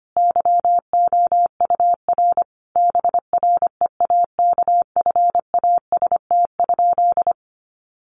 Morse code, heard as a keyed tone, about 25 wpm, 700 Hz. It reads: YOUR BREAKFAST?
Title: Sentences from Top 1500 Words 25wpm